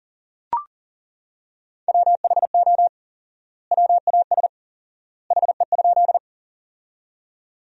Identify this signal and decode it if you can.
Morse code, heard as a keyed tone, about 40 wpm, 700 Hz. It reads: WHO WAS HE?
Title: 4x Word-Spacing - Sentences from Top 100 Words - 40wpm